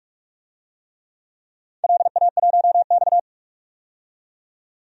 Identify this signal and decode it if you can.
Morse code, heard as a keyed tone, about 44 wpm, 690 Hz. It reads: LA1X
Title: DXCC-100 Call Signs - 4 Characters - 45wpm